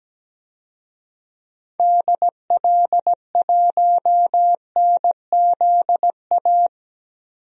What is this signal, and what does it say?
Morse code, heard as a keyed tone, about 17 wpm, 700 Hz. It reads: DL1NZA
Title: DXCC-100 Call Signs - 6 Characters - Repeated 3x - 17wpm